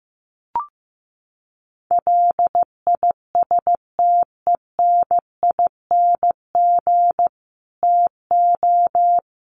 Morse code, heard as a keyed tone, about 15 wpm, 700 Hz. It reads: LISTENING TO
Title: Sets of 5 Words 15wpm